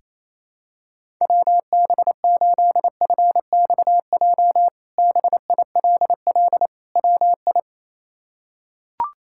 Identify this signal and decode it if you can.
Morse code, heard as a keyed tone, about 28 wpm, 700 Hz. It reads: WB8FXJ BILL WI
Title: K1USN SST QSO Practice - 28wpm